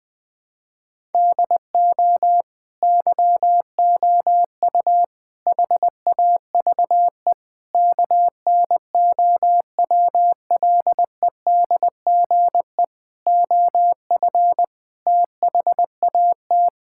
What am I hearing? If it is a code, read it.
Morse code, heard as a keyed tone, about 20 wpm, 700 Hz. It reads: DO YOU HAVE KNOWLEDGE OF THAT
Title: Sentences from Top 1000 Words - Review Part 6 - 20wpm